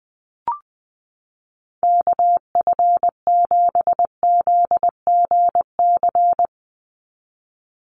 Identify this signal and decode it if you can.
Morse code, heard as a keyed tone, about 19 wpm, 700 Hz. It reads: KF7ZGC
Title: USA Call Signs 20wpm